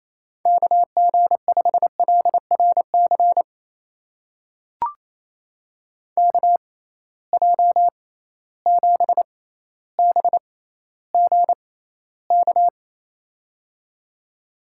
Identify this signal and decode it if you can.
Morse code, heard as a keyed tone, about 28 wpm, spaced as for 14 wpm, 700 Hz. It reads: KG5LRC KJ7BGK
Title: USA Call Signs - 6 Characters - 6x Character Spacing - 28wpm